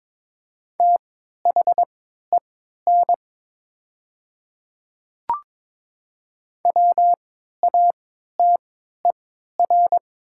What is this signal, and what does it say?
Morse code, heard as a keyed tone, about 22 wpm, spaced as for 12 wpm, 700 Hz. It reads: THEN WATER
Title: 3x Character Spacing - Top 100 Words - 22wpm